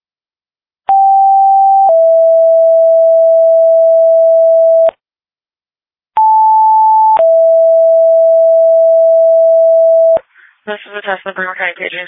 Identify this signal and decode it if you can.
Two-tone page, heard as a one-second tone, then a three-second tone, about 780 then 660 Hz, about 880 then 660 Hz, with speech. It is heard from 0.9 to 10.2 s.